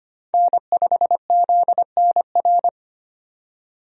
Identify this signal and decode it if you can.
Morse code, heard as a keyed tone, about 25 wpm, 700 Hz. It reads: N5ZNR
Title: USA Call Signs - 5 Characters - 25wpm